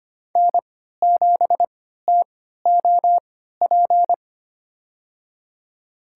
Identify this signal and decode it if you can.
Morse code, heard as a keyed tone, about 25 wpm, spaced as for 14 wpm, 700 Hz. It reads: N7TOP
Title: USA Call Signs - 3x Character Spacing - 25wpm